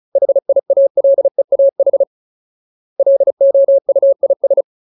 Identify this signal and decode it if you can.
Morse code, heard as a keyed tone, about 34 wpm, 550 Hz. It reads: HIALEAH LOUIS
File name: Ultimate Word List / 35WPM Ultimate